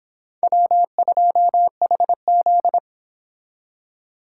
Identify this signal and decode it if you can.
Morse code, heard as a keyed tone, about 30 wpm, 700 Hz. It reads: W2HZ